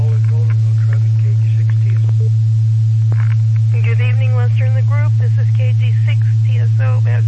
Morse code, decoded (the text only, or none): none